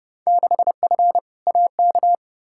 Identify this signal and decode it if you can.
Morse code, heard as a keyed tone, about 30 wpm, 700 Hz. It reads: 6F AK
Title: Field Day Exchanges Encore - 30wpm